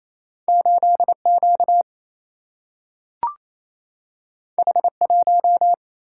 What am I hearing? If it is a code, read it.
Morse code, heard as a keyed tone, about 28 wpm, 700 Hz. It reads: 8Q H1
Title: Single-Letter Single-Number 28wpm